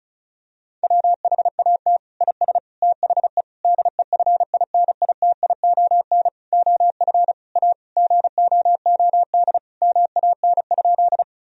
Morse code, heard as a keyed tone, about 36 wpm, 700 Hz. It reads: WHAT IS THE DEFINITION OF A GOOD MAN?
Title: Sentences from Top 1000 Words - Review Part 4 - 35wpm